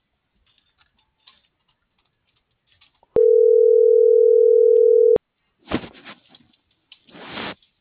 Ringback tone (North America), 440 + 480 Hz.